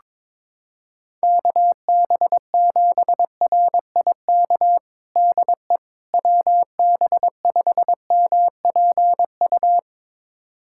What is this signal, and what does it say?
Morse code, heard as a keyed tone, about 21 wpm, 700 Hz. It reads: KB7RIK DE WB5MPU
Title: USA Callsign DE Callsign - 6 Characters - 22wpm